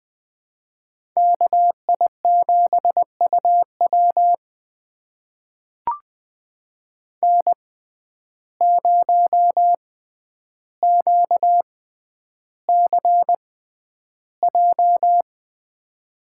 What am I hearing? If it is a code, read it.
Morse code, heard as a keyed tone, about 20 wpm, spaced as for 10 wpm, 700 Hz. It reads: KI7UW N0QCJ
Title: USA Call Signs - 5 Characters - 6x Character Spacing - 20wpm